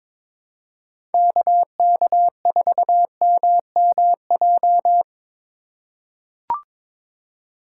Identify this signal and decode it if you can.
Morse code, heard as a keyed tone, about 22 wpm, 700 Hz. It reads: KK4MMJ
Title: USA Call Signs - 4x Character Spacing - 22wpm